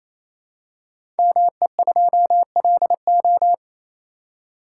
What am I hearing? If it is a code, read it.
Morse code, heard as a keyed tone, about 27 wpm, 700 Hz. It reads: ME2LO
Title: Code Groups - 5 Letter:Numbers - 28wpm